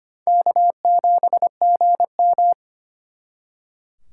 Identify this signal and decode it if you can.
Morse code, heard as a keyed tone, about 32 wpm, 700 Hz. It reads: K7GM